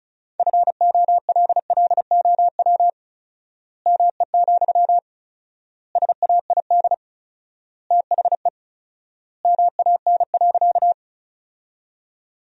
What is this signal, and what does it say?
Morse code, heard as a keyed tone, about 34 wpm, 700 Hz. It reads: FOLLOW ME, SAID THE MAN.
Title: 4x Word-Spacing - Sentences from Top 200 Words - 35wpm